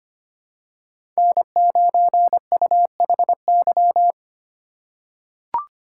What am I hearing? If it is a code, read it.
Morse code, heard as a keyed tone, about 25 wpm, 700 Hz. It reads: N9UHY
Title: USA Call Signs - 5 Characters - 3x Character Spacing - 25wpm